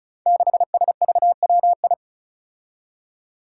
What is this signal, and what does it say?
Morse code, heard as a keyed tone, about 36 wpm, 700 Hz. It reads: 6SVWI
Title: Five Challenging Letter-Number Combinations 35wpm